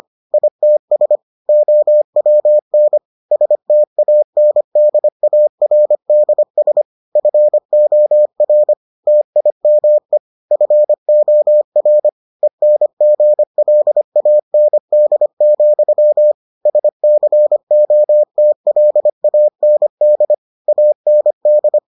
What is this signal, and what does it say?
Morse code, heard as a keyed tone, about 25 wpm, 600 Hz. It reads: ITS OWN STANDARDS FOR TIME FOR ENGLAND, SCOTLAND AND